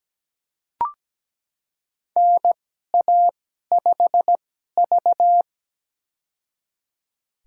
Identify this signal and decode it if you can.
Morse code, heard as a keyed tone, about 17 wpm, spaced as for 12 wpm, 700 Hz. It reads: NA5V
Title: USA Call Signs - 4 Characters - 2x Character Spacing - 17wpm